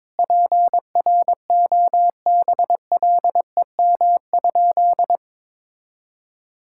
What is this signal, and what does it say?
Morse code, heard as a keyed tone, about 22 wpm, 700 Hz. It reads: PROBLEM?
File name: practice / 22wpm / Sentences with 8-Word Phrases - 22wpm